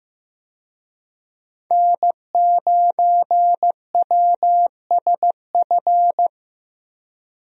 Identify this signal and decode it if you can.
Morse code, heard as a keyed tone, about 15 wpm, 700 Hz. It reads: N9WSF
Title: USA Call Signs - 5 Characters - 15wpm